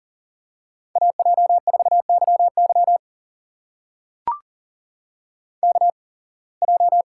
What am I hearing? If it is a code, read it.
Morse code, heard as a keyed tone, about 38 wpm, spaced as for 26 wpm, 700 Hz. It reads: AJ4YY KJ
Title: USA Call Signs - 5 Characters - 8x Character Spacing - 40wpm